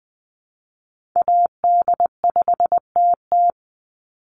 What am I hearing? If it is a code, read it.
Morse code, heard as a keyed tone, about 20 wpm, 700 Hz. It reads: AD5TT
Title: Call Signs - 20wpm